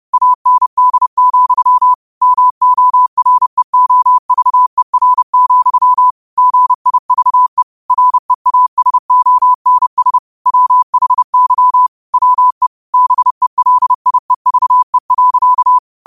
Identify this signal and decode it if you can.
Morse code, heard as a keyed tone, about 30 wpm, 1000 Hz. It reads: AND, MOREOVER, GIVE REASONS WHY WE BELIEVE.
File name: qotdfiles / 30wpm